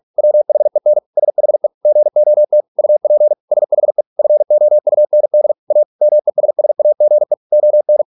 Morse code, heard as a keyed tone, about 46 wpm, 600 Hz. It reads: WHEN SHE GOT UP SHE FOUND A MESSAGE ON